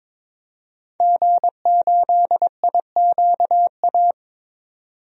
Morse code, heard as a keyed tone, about 21 wpm, 700 Hz. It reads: G8IQA
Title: DXCC-100 Call Signs - 5 Characters - 22wpm